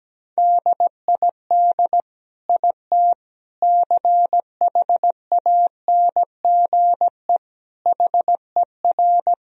Morse code, heard as a keyed tone, about 17 wpm, 700 Hz. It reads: DID IT CHANGE HER
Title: Sentences from Top 1000 Words - Review Part 1 - 17wpm